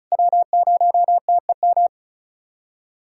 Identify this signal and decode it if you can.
Morse code, heard as a keyed tone, about 34 wpm, 690 Hz. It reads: W0TEM
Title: SOTA QSO Practice 35wpm